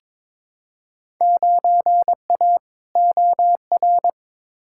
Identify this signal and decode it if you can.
Morse code, heard as a keyed tone, about 22 wpm, 700 Hz. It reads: 9A OR
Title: Field Day Exchanges Encore - 22wpm